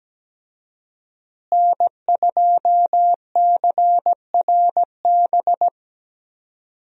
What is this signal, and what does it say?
Morse code, heard as a keyed tone, about 17 wpm, 700 Hz. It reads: N2CRB
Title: USA Call Signs - 5 Characters - Repeated 2x - 17wpm